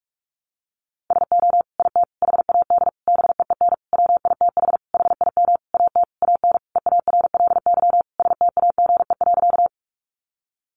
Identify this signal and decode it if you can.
Morse code, heard as a keyed tone, about 44 wpm, 700 Hz. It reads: SO IT HAD BEEN WITH HIM AT AN EARLY STAGE.
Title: Sentences from Top 1200 Words 45wpm